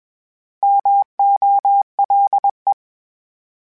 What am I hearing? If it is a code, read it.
Morse code, heard as a keyed tone, about 21 wpm, 800 Hz. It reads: MOLE
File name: Misc / Misc-mp3s / Misc @20